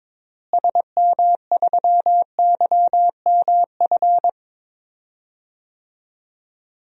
Morse code, heard as a keyed tone, about 22 wpm, 700 Hz. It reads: SM3YMF
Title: DXCC-100 Call Signs - 6 Characters - Repeated 2x - 22wpm